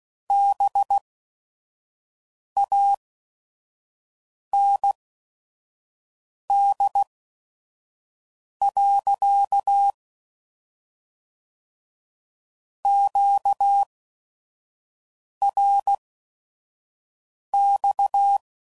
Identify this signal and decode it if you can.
Morse code, heard as a keyed tone, about 16 wpm, spaced as for 5 wpm, 800 Hz. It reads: BAND. QRX
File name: CW Code Groups @5WPM / QSOs / kc7sac